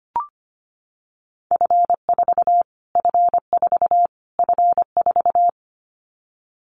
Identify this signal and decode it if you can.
Morse code, heard as a keyed tone, about 25 wpm, 700 Hz. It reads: F4 F4 F4
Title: Callsign Prefixes for Top 25 DXCC - Repeated 3x - 25wpm